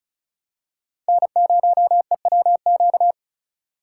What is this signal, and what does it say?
Morse code, heard as a keyed tone, about 34 wpm, 700 Hz. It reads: N0EWQ
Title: USA Call Signs - 5 Characters - 4x Character Spacing - 35wpm